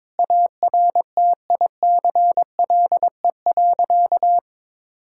Morse code, heard as a keyed tone, about 22 wpm, 700 Hz. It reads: ARTICLE.
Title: Sentences from Top 1000 Words - Review Part 5 - 22wpm